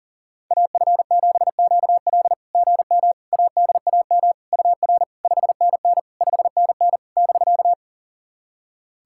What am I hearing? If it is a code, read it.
Morse code, heard as a keyed tone, about 40 wpm, 690 Hz. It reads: AF7QL GM ADAM UR 5NN 5NN <BK>
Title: SOTA QSO Practice 40wpm